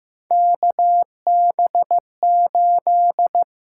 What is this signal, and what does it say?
Morse code, heard as a keyed tone, about 15 wpm, 700 Hz. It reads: KB8
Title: USA Call Signs - 6 Characters - 6x Character Spacing - 15wpm